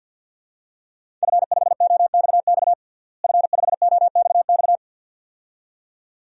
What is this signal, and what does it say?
Morse code, heard as a keyed tone, about 50 wpm, 700 Hz. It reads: F5OXX F5OXX
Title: DXCC-100 Call Signs - 5 Characters - Repeated 2x - 50wpm